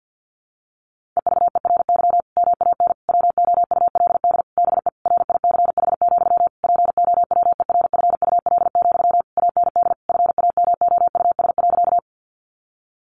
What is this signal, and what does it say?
Morse code, heard as a keyed tone, about 50 wpm, 700 Hz. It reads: EVERY MAN WOULD BE RICH, POWERFUL, AND FAMOUS.
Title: Sentences from Top 1500 Words 50wpm